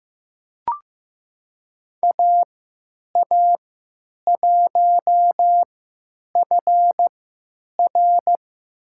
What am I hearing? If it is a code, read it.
Morse code, heard as a keyed tone, about 15 wpm, spaced as for 9 wpm, 700 Hz. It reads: AA1FR
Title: USA Call Signs - 5 Characters - 3x Character Spacing - 15wpm